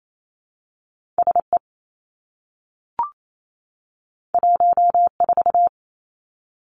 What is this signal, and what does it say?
Morse code, heard as a keyed tone, about 28 wpm, 700 Hz. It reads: S E 1 4
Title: Two Challenging Letter-Number Combinations 28wpm